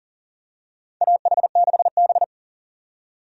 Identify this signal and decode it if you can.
Morse code, heard as a keyed tone, about 40 wpm, 700 Hz. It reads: AH6B
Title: USA Call Signs - 4 Characters - 40wpm